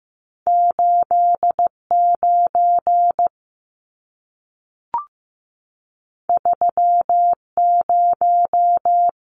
Morse code, heard as a keyed tone, about 15 wpm, 700 Hz. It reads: 8 9 3 0
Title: Numbers - 2-Digit 15wpm